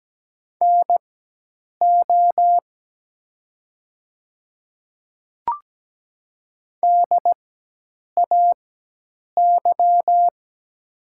Morse code, heard as a keyed tone, about 17 wpm, spaced as for 8 wpm, 700 Hz. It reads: NO DAY
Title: 4x Character Spacing - Top 200 Words - 17wpm